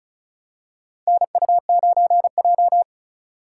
Morse code, heard as a keyed tone, about 36 wpm, 700 Hz. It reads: NU9J
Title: USA Call Signs - 2x Character Spacing - 35wpm